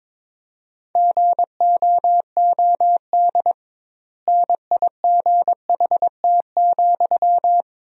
Morse code, heard as a keyed tone, about 22 wpm, 700 Hz. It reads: GOOD NIGHT,
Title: 2x Word-Spacing - Sentences from Top 300 Words - 22wpm